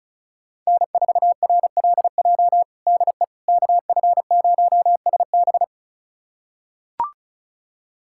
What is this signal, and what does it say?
Morse code, heard as a keyed tone, about 34 wpm, 700 Hz. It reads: N4RLJ DE KF0SB
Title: USA Callsign DE Callsign - 5 Characters - 35wpm